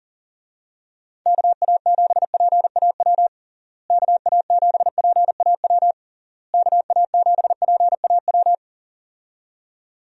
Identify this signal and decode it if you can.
Morse code, heard as a keyed tone, about 40 wpm, 700 Hz. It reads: KA7PAW KA7PAW KA7PAW
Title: USA Call Signs - 6 Characters - Repeated 3x - 40wpm